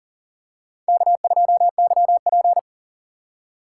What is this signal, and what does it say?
Morse code, heard as a keyed tone, about 40 wpm, 700 Hz. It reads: K2YP
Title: USA Call Signs - 4x Character Spacing - 40wpm